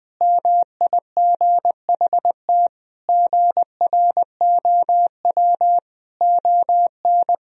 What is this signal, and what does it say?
Morse code, heard as a keyed tone, about 21 wpm, 700 Hz. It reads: MIGHT GROW ON
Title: Sentences from Top 1000 Words - Review Part 1 - 20wpm